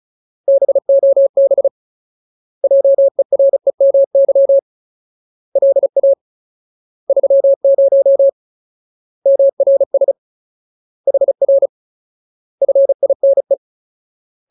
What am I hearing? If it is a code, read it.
Morse code, heard as a keyed tone, about 34 wpm, 550 Hz. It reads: BOB JEREMY LA 30 MRS HR FINE
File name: Ultimate Word List / 35WPM Ultimate